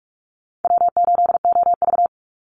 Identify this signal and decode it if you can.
Morse code, heard as a keyed tone, about 44 wpm, 700 Hz. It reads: W8OV
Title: Easy Call Signs Repeated 2x - 45wpm